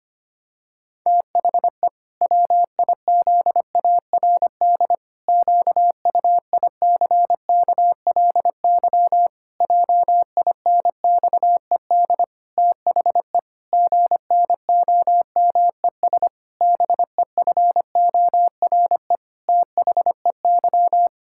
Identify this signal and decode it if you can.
Morse code, heard as a keyed tone, about 25 wpm, 700 Hz. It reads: THE WIZARD QUICKLY JINXED THE GNOMES BEFORE THEY
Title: Pangram Challenge v2 - 25wpm